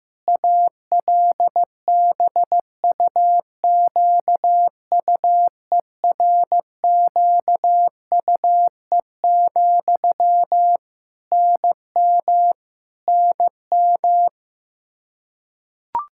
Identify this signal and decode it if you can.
Morse code, heard as a keyed tone, about 16 wpm, 700 Hz. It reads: ALBUQUERQUE, NM NM
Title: Practice QSO - Exchange 1 - Beginning - 15wpm